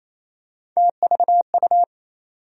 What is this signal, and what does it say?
Morse code, heard as a keyed tone, about 28 wpm, 700 Hz. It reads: TVU
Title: USA Call Sign Postfixes - 3 Letters 28wpm